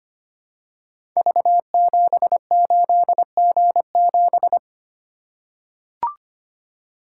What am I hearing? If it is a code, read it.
Morse code, heard as a keyed tone, about 25 wpm, 700 Hz. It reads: V78G7
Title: Code Groups - 5 Letter:Numbers - 25wpm